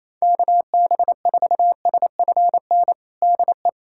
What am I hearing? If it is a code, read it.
Morse code, heard as a keyed tone, about 27 wpm, 700 Hz. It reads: KB4SFN DE
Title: USA Callsign DE Callsign - 6 Characters - 28wpm